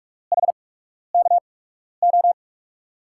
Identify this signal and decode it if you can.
Morse code, heard as a keyed tone, about 45 wpm, spaced as for 12 wpm, 700 Hz. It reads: HKO
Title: USA Call Signs - 6 Characters - 8x Character Spacing - 45wpm